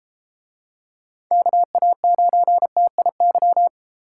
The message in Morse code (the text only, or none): KA9TIY